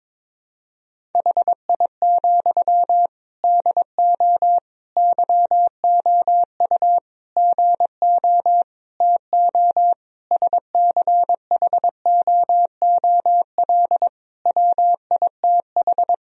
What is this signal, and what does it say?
Morse code, heard as a keyed tone, about 22 wpm, 700 Hz. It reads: HI, DO YOU GO TO SCHOOL WITH
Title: Sentences from Top 1000 Words - Review Part 3 - 22wpm